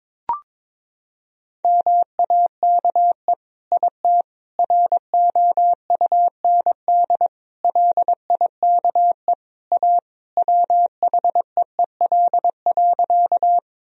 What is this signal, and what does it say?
Morse code, heard as a keyed tone, about 22 wpm, 700 Hz. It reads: MAKE IT ROUND LIKE A WHEEL.
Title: Sentences from Top 1000 Words - Review Part 3 - 22wpm